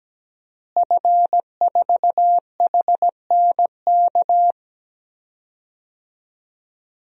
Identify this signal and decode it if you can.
Morse code, heard as a keyed tone, about 17 wpm, 700 Hz. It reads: F4HNK
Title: DXCC-100 Call Signs - 5 Characters - Repeated 2x - 17wpm